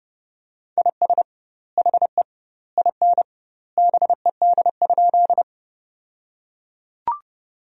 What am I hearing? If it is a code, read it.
Morse code, heard as a keyed tone, about 32 wpm, 700 Hz. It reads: IS HE IN BED?
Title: 2x Word-Spacing - Sentences from Top 500 Words - 30wpm